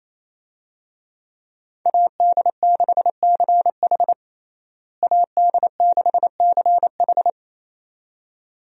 Morse code, heard as a keyed tone, about 27 wpm, 700 Hz. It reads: AD6CH AD6CH
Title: USA Call Signs - 5 Characters - Repeated 2x - 28wpm